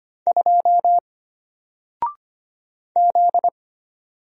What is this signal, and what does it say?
Morse code, heard as a keyed tone, about 25 wpm, 700 Hz. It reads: 2 Z
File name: Single-Letter-Number-25wpm